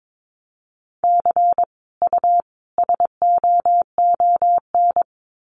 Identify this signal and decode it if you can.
Morse code, heard as a keyed tone, about 22 wpm, 700 Hz. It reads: C U SOON
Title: Common QSO Phrases 22wpm